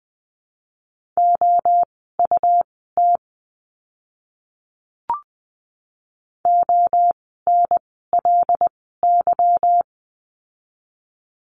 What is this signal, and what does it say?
Morse code, heard as a keyed tone, about 20 wpm, spaced as for 15 wpm, 700 Hz. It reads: OUT ONLY
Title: 206 Lesson 33f - Letter j - Words 20wpm